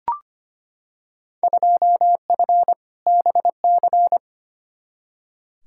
Morse code, heard as a keyed tone, about 25 wpm, 700 Hz. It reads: 2F BC
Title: Field Day Exchanges Encore - 25wpm